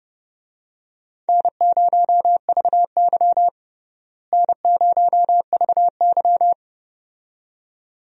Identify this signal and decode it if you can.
Morse code, heard as a keyed tone, about 32 wpm, 700 Hz. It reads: N0VY N0VY
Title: USA Call Signs - 4 Characters - Repeated 2x - 30wpm